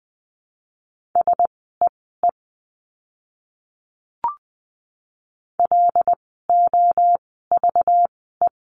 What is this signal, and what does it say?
Morse code, heard as a keyed tone, about 20 wpm, spaced as for 14 wpm, 700 Hz. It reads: SEE LOVE
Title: 192 Lesson 31f - Letter v - Words 20wpm